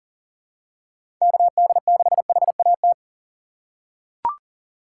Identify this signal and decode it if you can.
Morse code, heard as a keyed tone, about 40 wpm, 700 Hz. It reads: KD6HAT
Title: USA Call Signs - 3x Character Spacing - 40wpm